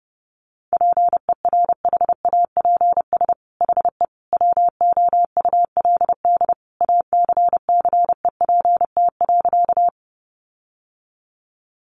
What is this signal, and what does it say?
Morse code, heard as a keyed tone, about 30 wpm, 700 Hz. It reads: PERHAPS HE WOULD ACCEPT.
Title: Sentences from Top 1300 Words 30wpm